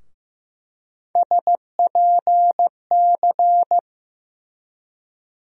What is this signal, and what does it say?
Morse code, heard as a keyed tone, about 15 wpm, 700 Hz. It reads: SPC
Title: USA Call Sign Postfixes - 3 Letters 15wpm